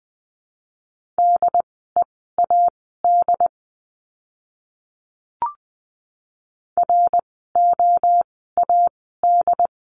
Morse code, heard as a keyed tone, about 20 wpm, spaced as for 14 wpm, 700 Hz. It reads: DEAD ROAD
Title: 075 Lesson 13f - Letter L - Words 20wpm